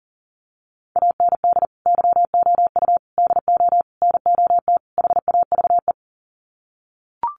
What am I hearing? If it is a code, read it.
Morse code, heard as a keyed tone, about 37 wpm, 700 Hz. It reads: AND YOU DO NOT HAVE
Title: Sets of 5 Words Encore 40wpm